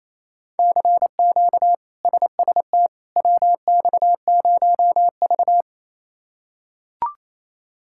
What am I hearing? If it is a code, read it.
Morse code, heard as a keyed tone, about 28 wpm, 700 Hz. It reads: CQ SST WX0V
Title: K1USN SST QSO Practice - 28wpm